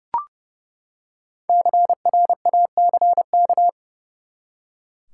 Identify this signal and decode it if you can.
Morse code, heard as a keyed tone, about 30 wpm, 700 Hz. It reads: CRACK
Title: Five Letter Words - 30wpm